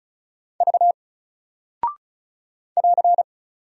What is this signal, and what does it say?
Morse code, heard as a keyed tone, about 36 wpm, 700 Hz. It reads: V <AR>
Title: Single Character (Letter:Number:Punctuation:Prosign) 35wpm